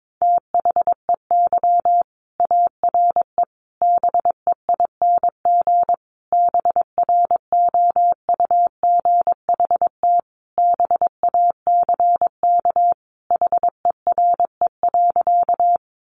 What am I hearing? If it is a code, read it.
Morse code, heard as a keyed tone, about 22 wpm, 700 Hz. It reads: THEY ARE BEING BROUGHT BACK HERE.
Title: Sentences from Top 600 Words 22wpm